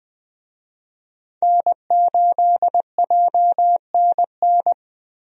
Morse code, heard as a keyed tone, about 20 wpm, 700 Hz. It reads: N8JNN
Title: USA Call Signs - 5 Characters - Repeated 3x - 20wpm